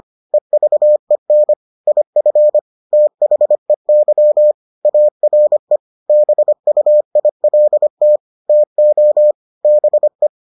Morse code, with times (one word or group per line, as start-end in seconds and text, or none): 0.3-1.5 EVEN
1.9-2.6 IF
2.9-4.5 THEY
4.8-5.8 ARE
6.1-8.2 BUILT
8.5-9.3 TO
9.6-10.3 BE